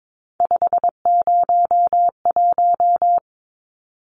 Morse code, heard as a keyed tone, about 22 wpm, 700 Hz. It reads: 501
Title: Numbers - 3-Digit 22wpm